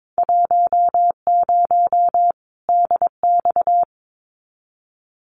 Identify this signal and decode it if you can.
Morse code, heard as a keyed tone, about 22 wpm, 700 Hz. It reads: 10 DX
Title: Winter Field Day Exchanges 22wpm